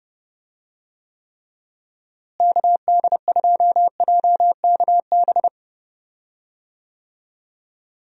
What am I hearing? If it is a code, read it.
Morse code, heard as a keyed tone, about 31 wpm, 700 Hz. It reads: KD2JKB
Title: USA Call Signs - 6 Characters - Repeated 2x - 30wpm